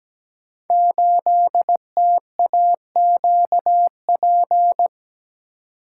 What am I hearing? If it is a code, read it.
Morse code, heard as a keyed tone, about 17 wpm, 700 Hz. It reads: 8TAQP
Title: Code Groups - 5 Letter:Numbers - 17wpm